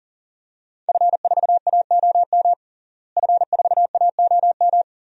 Morse code, heard as a keyed tone, about 40 wpm, 700 Hz. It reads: F4AOM F4AOM
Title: DXCC-100 Call Signs - 5 Characters - Repeated 2x - 40wpm